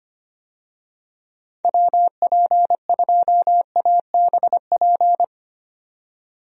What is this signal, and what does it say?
Morse code, heard as a keyed tone, about 25 wpm, 700 Hz. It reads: WP2ABP
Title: USA Call Signs - 6 Characters - 25wpm